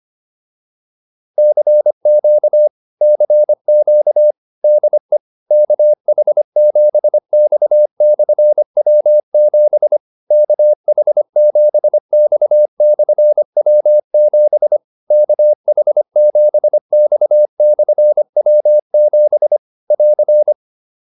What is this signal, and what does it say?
Morse code, heard as a keyed tone, about 25 wpm, 600 Hz. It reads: CQ CQ DE KH7X/W7 KH7X/W7 KH7X/W7 <AR>